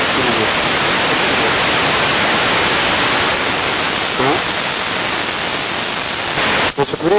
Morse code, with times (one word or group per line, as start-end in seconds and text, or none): none